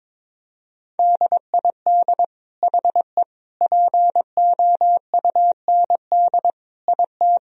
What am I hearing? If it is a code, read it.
Morse code, heard as a keyed tone, about 22 wpm, 700 Hz. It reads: DID HE POUND IT
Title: Sentences from Top 1000 Words - Review Part 2 - 22wpm